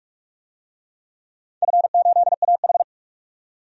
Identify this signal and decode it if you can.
Morse code, heard as a keyed tone, about 42 wpm, 700 Hz. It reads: F8AH
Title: DXCC-100 Call Signs - 4 Characters - 45wpm